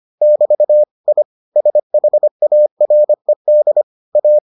Morse code, heard as a keyed tone, about 25 wpm, 600 Hz. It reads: <BT> I SHARED A